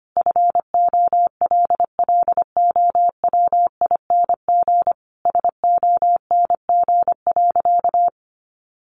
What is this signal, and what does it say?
Morse code, heard as a keyed tone, about 25 wpm, 700 Hz. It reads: FOLLOWING SONG.
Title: Sentences from Top 900 Words 25wpm